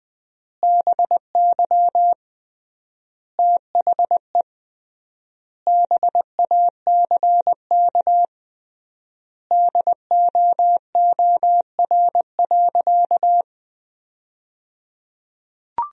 Morse code, heard as a keyed tone, about 20 wpm, 700 Hz. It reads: BY THE BACK DOOR.
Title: 3x Word-Spacing - Sentences from Top 300 Words - 20wpm